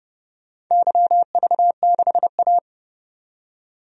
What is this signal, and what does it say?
Morse code, heard as a keyed tone, about 32 wpm, 700 Hz. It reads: YV6A
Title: DXCC-100 Call Signs - 4 Characters - 30wpm